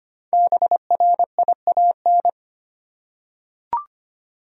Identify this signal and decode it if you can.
Morse code, heard as a keyed tone, about 25 wpm, 700 Hz. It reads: BRIAN
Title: USA Common Names - 25wpm